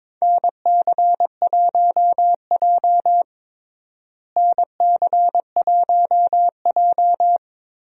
Morse code, heard as a keyed tone, about 22 wpm, 700 Hz. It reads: NC1J NC1J
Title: USA Call Signs - 4 Characters - Repeated 3x - 22wpm